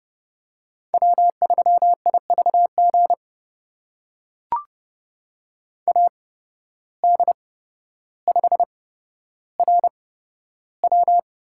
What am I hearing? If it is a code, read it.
Morse code, heard as a keyed tone, about 31 wpm, spaced as for 13 wpm, 700 Hz. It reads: W3IVG AD5RW
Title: USA Call Signs - 5 Characters - 8x Character Spacing - 30wpm